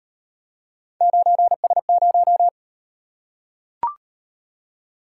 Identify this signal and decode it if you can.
Morse code, heard as a keyed tone, about 37 wpm, 700 Hz. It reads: 9S0